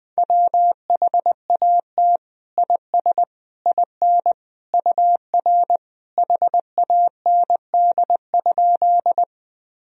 Morse code, heard as a keyed tone, about 20 wpm, 700 Hz. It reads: WHAT IS IN UR HAND?
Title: Sentences from Top 200 Words and Abbreviations 20wpm